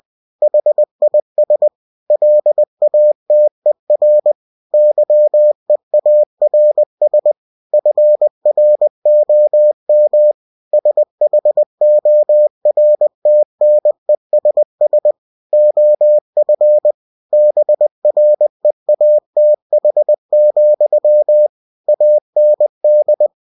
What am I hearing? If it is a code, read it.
Morse code, heard as a keyed tone, about 20 wpm, 600 Hz. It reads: HIS LATER YEARS FROM SHORTNESS OF BREATH, AND
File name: cw-books / winnie-the-pooh / winnie-20wpm-page-0022